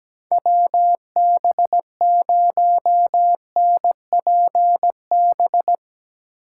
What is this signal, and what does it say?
Morse code, heard as a keyed tone, about 17 wpm, 700 Hz. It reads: WB0NPB
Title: USA Call Signs - 6 Characters - 8x Character Spacing - 17wpm